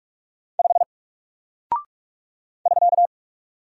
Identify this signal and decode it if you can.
Morse code, heard as a keyed tone, about 44 wpm, 700 Hz. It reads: 5 <SK>
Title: Single Character (Letter:Number:Punctuation:Prosign) 45wpm